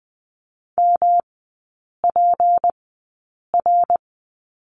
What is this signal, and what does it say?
Morse code, heard as a keyed tone, about 20 wpm, spaced as for 8 wpm, 700 Hz. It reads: MPR
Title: 152 Lesson 25e - Letter p - Three Characters 20wpm